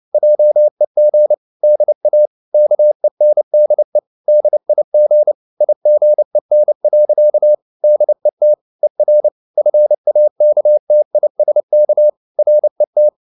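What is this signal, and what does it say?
Morse code, heard as a keyed tone, about 29 wpm, 600 Hz. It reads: JEG DA KENDE DIG IGEN. DET ER FAKTISK RET
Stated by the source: Fast (QRQ) Morse at 29 wpm, from CW med Gnister.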